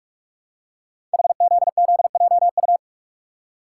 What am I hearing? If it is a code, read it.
Morse code, heard as a keyed tone, about 44 wpm, 700 Hz. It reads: HZZJU
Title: Code Groups - 5 Letters - 45wpm